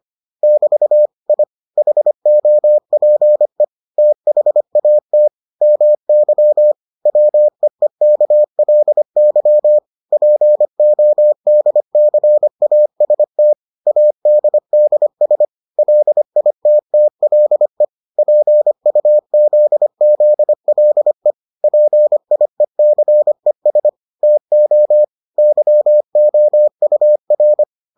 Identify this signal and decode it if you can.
Morse code, heard as a keyed tone, about 25 wpm, 600 Hz. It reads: <BT> I HOPE THAT MY WEEKLY PODCAST ADDS LITTLE PUZZLE PIECES TO YOUR